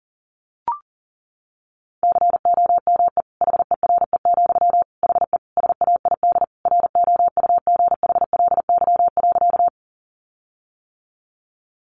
Morse code, heard as a keyed tone, about 40 wpm, 700 Hz. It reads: COME HERE, HE SAID ROUGHLY.
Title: Sentences from Top 1500 Words 40wpm